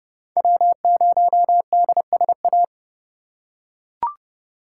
Morse code, heard as a keyed tone, about 31 wpm, 700 Hz. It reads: W0DSA
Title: USA Call Signs - 3x Character Spacing - 30wpm